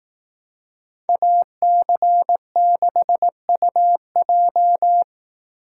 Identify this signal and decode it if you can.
Morse code, heard as a keyed tone, about 18 wpm, 700 Hz. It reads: AC6UJ
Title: Speed Racing - USA 2x2 Call Signs 15wpm